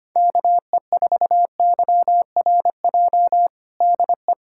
Morse code, heard as a keyed tone, about 25 wpm, 700 Hz. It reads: KE4YRJ DE
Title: USA Callsign DE Callsign - 6 Characters - 25wpm